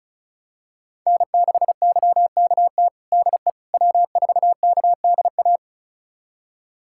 Morse code, heard as a keyed tone, about 34 wpm, 700 Hz. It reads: N6YKT DE W4KDA
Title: USA Callsign DE Callsign - 5 Characters - 35wpm